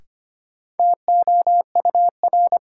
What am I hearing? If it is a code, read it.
Morse code, heard as a keyed tone, about 25 wpm, 700 Hz. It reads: TOUR
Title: Sentences from Top 1400 Words - New Word Drill - 25wpm